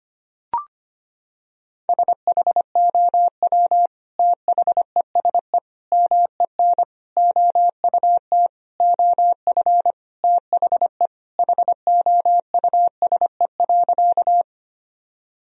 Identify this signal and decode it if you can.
Morse code, heard as a keyed tone, about 25 wpm, 700 Hz. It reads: SHOW THESE MEN OUT OF THE HOUSE.
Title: Sentences from Top 1000 Words - Review Part 1 - 25wpm